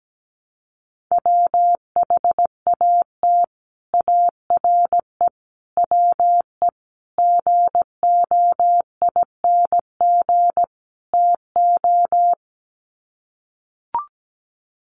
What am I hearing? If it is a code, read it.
Morse code, heard as a keyed tone, about 17 wpm, 700 Hz. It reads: WHAT ARE WE GOING TO
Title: Sets of 5 Words 17wpm